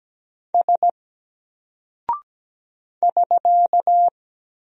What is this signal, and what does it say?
Morse code, heard as a keyed tone, about 17 wpm, 700 Hz. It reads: S <SK>
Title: Single Character (Letter:Number:Punctuation:Prosign) 17wpm